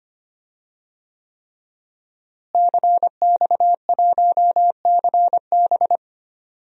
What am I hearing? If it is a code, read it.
Morse code, heard as a keyed tone, about 25 wpm, 700 Hz. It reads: CX1CB
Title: DXCC-100 Call Signs - 5 Characters - Repeated 3x - 25wpm